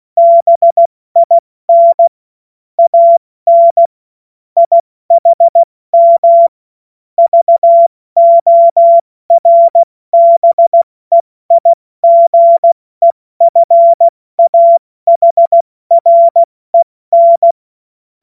Morse code, heard as a keyed tone, about 16 wpm, 680 Hz. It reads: BIN AN IHM VORBEIGEFAHREN
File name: Morsekurs / Witze / Witz 31. Mai 2022 70 F80